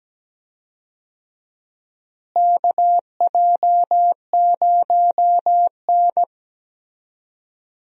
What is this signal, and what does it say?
Morse code, heard as a keyed tone, about 17 wpm, 700 Hz. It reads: KJ0N
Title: USA Call Signs - 4 Characters - Repeated 3x - 17wpm